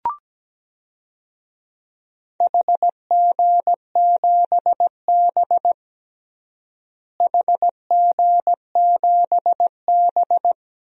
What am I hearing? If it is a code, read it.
Morse code, heard as a keyed tone, about 17 wpm, 700 Hz. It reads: HG7B HG7B
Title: DXCC-100 Call Signs - 4 Characters - Repeated 2x - 17wpm